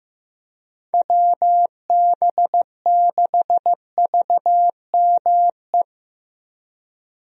Morse code, heard as a keyed tone, about 15 wpm, 700 Hz. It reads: WB6VME
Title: USA Call Signs - 6 Characters - Repeated 2x - 15wpm